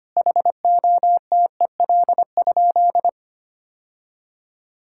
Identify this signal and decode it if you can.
Morse code, heard as a keyed tone, about 25 wpm, 700 Hz. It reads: HOTEL?
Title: Sentences from Top 1600 Words 25wpm